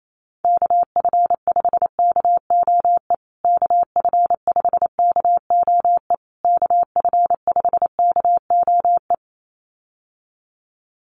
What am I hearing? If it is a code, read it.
Morse code, heard as a keyed tone, about 28 wpm, 690 Hz. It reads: KF5KOE KF5KOE KF5KOE
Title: USA Call Signs Repeated 3x 28wpm